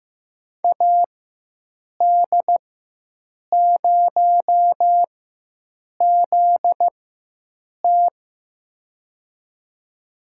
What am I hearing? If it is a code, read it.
Morse code, heard as a keyed tone, about 15 wpm, spaced as for 7 wpm, 700 Hz. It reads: AD0ZT
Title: USA Call Signs - 4x Character Spacing - 15wpm